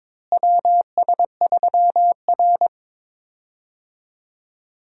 Morse code, heard as a keyed tone, about 22 wpm, 700 Hz. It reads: WS3R
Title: USA Call Signs - 4 Characters - Repeated 3x - 22wpm